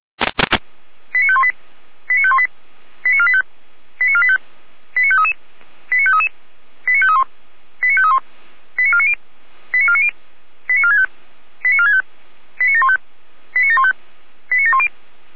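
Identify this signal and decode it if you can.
Five-tone selective call (ZVEI), heard as quick runs of five tones, tones of about 70 ms. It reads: 89518 89518 89485 89485 89533 89533 89531 89531 89490 89490 89465 89465 89715 89715 89710
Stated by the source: ZVEI five-tone selective call 89042 received on an SDR.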